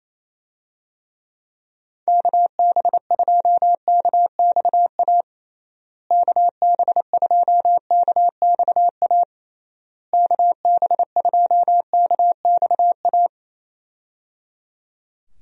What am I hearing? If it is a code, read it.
Morse code, heard as a keyed tone, about 27 wpm, 700 Hz. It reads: KB2KXA KB2KXA KB2KXA
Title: USA Call Signs - 6 Characters - Repeated 3x - 28wpm